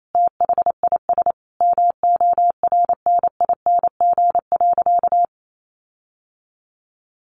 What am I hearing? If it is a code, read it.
Morse code, heard as a keyed tone, about 28 wpm, 700 Hz. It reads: THIS MORNING.
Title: Sentences from Top 1500 Words 28wpm